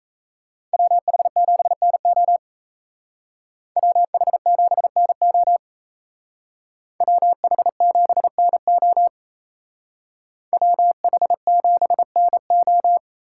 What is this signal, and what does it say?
Morse code, heard as a keyed tone, about 36 wpm, 700 Hz. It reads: WH7NO WH7NO WH7NO WH7NO